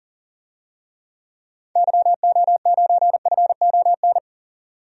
Morse code, heard as a keyed tone, about 40 wpm, 700 Hz. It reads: YO9FON
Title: DXCC-100 Call Signs - 6 Characters - 40wpm